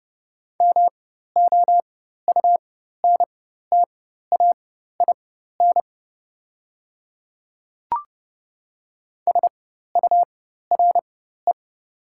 Morse code, heard as a keyed tone, about 30 wpm, spaced as for 14 wpm, 700 Hz. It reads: MOUNTAIN SURE
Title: 4x Character Spacing - Top 300 Words - 30wpm